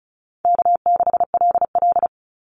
Morse code, heard as a keyed tone, about 34 wpm, 700 Hz. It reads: K6LL
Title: Call Signs - 35wpm